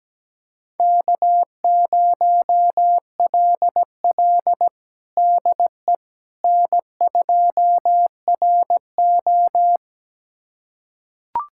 Morse code, heard as a keyed tone, about 17 wpm, 700 Hz. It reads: K0LL DE N2RO
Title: USA Callsign DE Callsign - 4 Characters - 17wpm